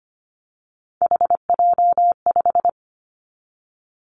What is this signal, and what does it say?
Morse code, heard as a keyed tone, about 25 wpm, 700 Hz. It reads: HJ5
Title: Callsign Prefixes for Top 100 DXCC - 25wpm